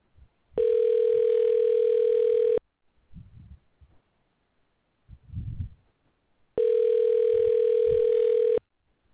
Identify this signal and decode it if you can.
Ringback tone (North America), 440 + 480 Hz.